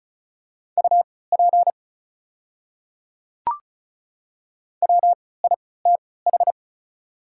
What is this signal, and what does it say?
Morse code, heard as a keyed tone, about 36 wpm, spaced as for 20 wpm, 690 Hz. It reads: UP WITH
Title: 3x Character Spacing - Top 100 Words - 35wpm